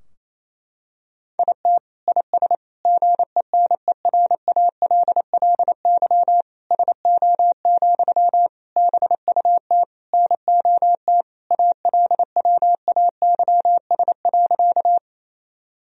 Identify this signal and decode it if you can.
Morse code, heard as a keyed tone, about 28 wpm, 700 Hz. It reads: IT IS GENERALLY SO, BUT NOT ALWAYS.
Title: Sentences from Top 1000 Words - Review Part 4 - 28wpm